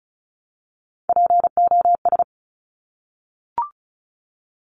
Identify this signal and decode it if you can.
Morse code, heard as a keyed tone, about 36 wpm, 700 Hz. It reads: POS
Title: Top Three Letter Combinations in Words 35wpm